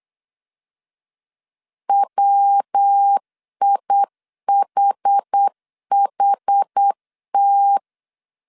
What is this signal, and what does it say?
Morse code, heard as a keyed tone, about 8 wpm, 800 Hz. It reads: WIHHT
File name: CW / DET / 1 3 1500C S B 8wpm 4141